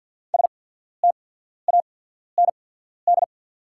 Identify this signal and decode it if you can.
Morse code, heard as a keyed tone, about 50 wpm, spaced as for 14 wpm, 700 Hz. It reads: STAND
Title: 8x Character Spacing - Top 500 Words - 50wpm